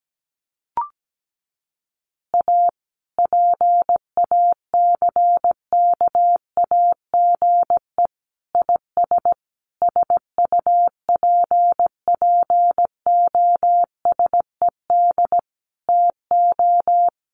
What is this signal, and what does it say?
Morse code, heard as a keyed tone, about 17 wpm, 700 Hz. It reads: A PACKAGE IS SUPPOSED TO
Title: Easy Phrases 17wpm